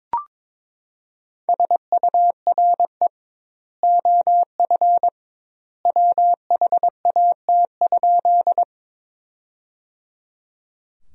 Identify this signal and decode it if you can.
Morse code, heard as a keyed tone, about 21 wpm, 700 Hz. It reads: SURE OF WHAT?
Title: 2x Word-Spacing - Sentences from Top 300 Words - 22wpm